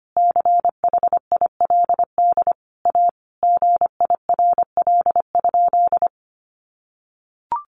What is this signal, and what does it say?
Morse code, heard as a keyed tone, about 25 wpm, 700 Hz. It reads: CHILD A GIRL?
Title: Sentences from Top 800 Words 25wpm